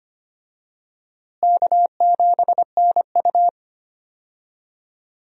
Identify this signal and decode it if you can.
Morse code, heard as a keyed tone, about 25 wpm, 700 Hz. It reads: K7NU
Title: USA Call Signs - 4 Characters - Repeated 3x - 25wpm